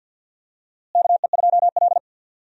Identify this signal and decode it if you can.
Morse code, heard as a keyed tone, about 50 wpm, 700 Hz. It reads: KE2L
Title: USA Call Signs - 4 Characters - Repeated 3x - 50wpm